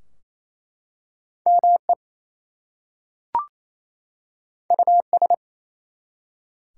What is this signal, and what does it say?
Morse code, heard as a keyed tone, about 27 wpm, 700 Hz. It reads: ME US
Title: Two Letter Words - 28wpm